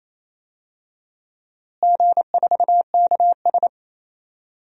Morse code, heard as a keyed tone, about 28 wpm, 700 Hz. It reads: G4KS
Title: DXCC-100 Call Signs - 4 Characters - Repeated 2x - 28wpm